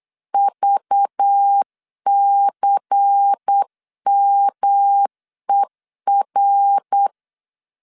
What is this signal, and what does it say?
Morse code, heard as a keyed tone, about 8 wpm, 800 Hz. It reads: VCMER
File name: CW / DET / 1 6 1500C S B 8wpm 5142